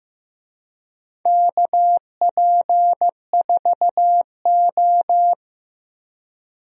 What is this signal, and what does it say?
Morse code, heard as a keyed tone, about 15 wpm, 700 Hz. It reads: KP4O
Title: DXCC-100 Call Signs - 4 Characters - Repeated 3x - 15wpm